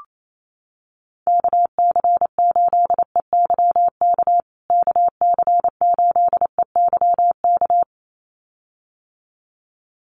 Morse code, heard as a keyed tone, about 28 wpm, 700 Hz. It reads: KC8EYK KC8EYK
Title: USA Call Signs Repeated 2x 28wpm